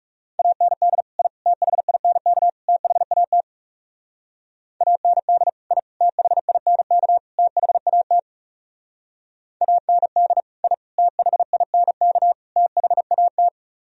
Morse code, heard as a keyed tone, about 37 wpm, 700 Hz. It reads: AND I THINK THAT AND I THINK THAT AND I THINK THAT
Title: Speed-Racing - Sets of 4 Words 30wpm